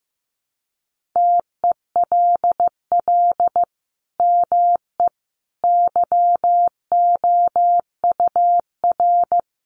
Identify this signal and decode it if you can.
Morse code, heard as a keyed tone, about 15 wpm, 700 Hz. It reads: TELL ME YOUR
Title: Sentences from Top 1400 Words 15wpm